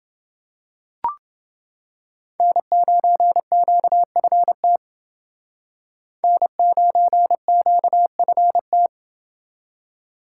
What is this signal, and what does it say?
Morse code, heard as a keyed tone, about 29 wpm, 700 Hz. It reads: N9QFT N9QFT